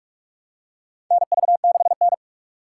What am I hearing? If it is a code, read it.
Morse code, heard as a keyed tone, about 44 wpm, 700 Hz. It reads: NV6N
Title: USA Call Signs - 4 Characters - 2x Character Spacing - 45wpm